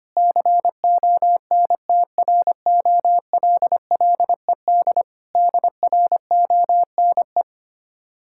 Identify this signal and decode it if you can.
Morse code, heard as a keyed tone, about 25 wpm, 700 Hz. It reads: CONTROLLED DRONE
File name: practice / 25wpm / Difficult Words with Hints - 25wpm